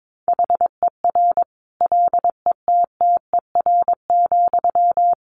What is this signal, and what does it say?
Morse code, heard as a keyed tone, about 22 wpm, 700 Hz. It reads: HER LETTER,
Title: Sentences from Top 1100 Words 22wpm